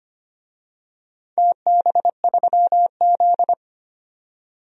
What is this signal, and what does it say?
Morse code, heard as a keyed tone, about 25 wpm, 700 Hz. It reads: TB3Z
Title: DXCC-100 Call Signs - 4 Characters - 25wpm